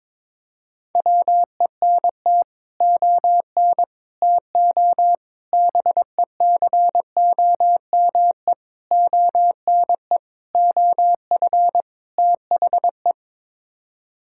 Morse code, heard as a keyed tone, about 22 wpm, 700 Hz. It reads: WENT ON TO BECOME ONE OF THE
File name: practice / 22wpm / Sets of 7 Words - 22wpm